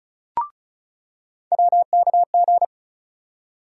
Morse code, heard as a keyed tone, about 36 wpm, 700 Hz. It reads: WKG
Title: USA Call Sign Postfixes - 3 Letters 35wpm